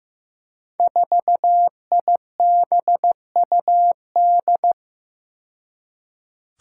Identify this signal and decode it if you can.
Morse code, heard as a keyed tone, about 15 wpm, 700 Hz. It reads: 4IBUD
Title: Five Challenging Letter-Number Combinations 15wpm